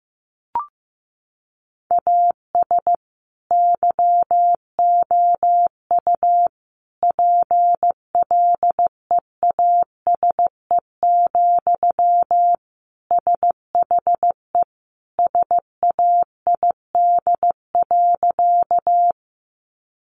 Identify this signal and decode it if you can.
Morse code, heard as a keyed tone, about 15 wpm, 700 Hz. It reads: AS YOU PLEASE, SHE SAID.
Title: Sentences from Top 1500 Words 15wpm